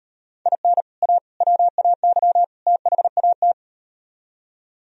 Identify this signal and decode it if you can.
Morse code, heard as a keyed tone, about 38 wpm, 700 Hz. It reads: IN A WAY THAT